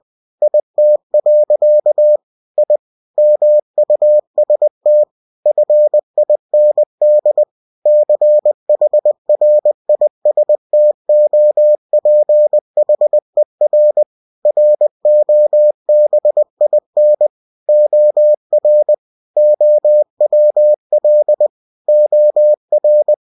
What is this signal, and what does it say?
Morse code, heard as a keyed tone, about 20 wpm, 600 Hz. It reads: IT. I MUST FIND CHRISTOPHER ROBIN OR OWL OR